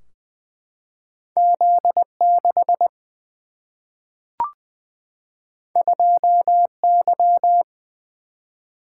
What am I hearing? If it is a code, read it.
Morse code, heard as a keyed tone, about 20 wpm, 700 Hz. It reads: Z 6 2 Y
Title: Single-Letter Single-Number 20wpm